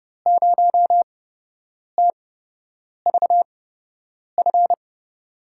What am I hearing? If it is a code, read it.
Morse code, heard as a keyed tone, about 30 wpm, spaced as for 8 wpm, 700 Hz. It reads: 0TVF
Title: USA Call Signs - 6 Characters - 8x Character Spacing - 30wpm